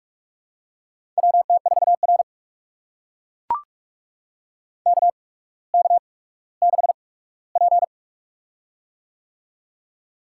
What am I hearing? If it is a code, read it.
Morse code, heard as a keyed tone, about 42 wpm, spaced as for 19 wpm, 700 Hz. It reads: WT4R KK6P
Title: USA Call Signs - 4 Characters - 8x Character Spacing - 45wpm